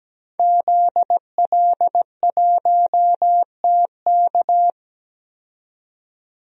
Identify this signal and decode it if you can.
Morse code, heard as a keyed tone, about 17 wpm, 700 Hz. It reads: ZL1TK
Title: DXCC-100 Call Signs - 5 Characters - 17wpm